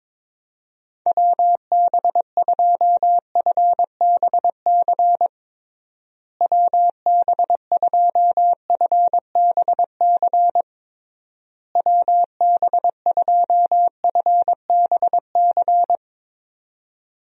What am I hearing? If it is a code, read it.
Morse code, heard as a keyed tone, about 22 wpm, 700 Hz. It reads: WB2FBC WB2FBC WB2FBC
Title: USA Call Signs - 6 Characters - Repeated 3x - 22wpm